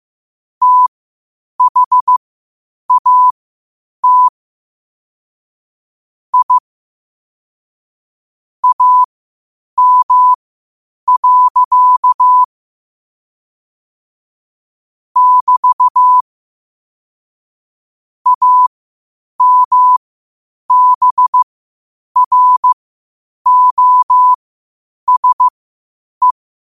Morse code, heard as a keyed tone, about 15 wpm, spaced as for 8 wpm, 1000 Hz. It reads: THAT I AM. <BT> AMBROSE
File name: qotdfiles / 8wpm